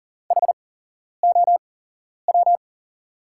Morse code, heard as a keyed tone, about 40 wpm, spaced as for 11 wpm, 690 Hz. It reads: HOW
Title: 8x Character Spacing - Top 300 Words - 40wpm